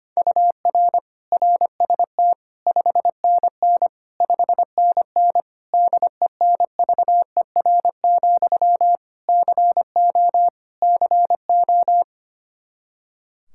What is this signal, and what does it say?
Morse code, heard as a keyed tone, about 25 wpm, 700 Hz. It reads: UR RST 5NN 5NN DENVER, CO CO
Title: Practice QSO - Exchange 1 - Beginning - 25wpm